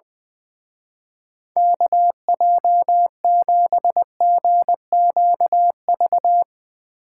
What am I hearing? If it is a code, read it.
Morse code, heard as a keyed tone, about 22 wpm, 700 Hz. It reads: KJ7GQV
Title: Speed Racing - USA Call Signs 20wpm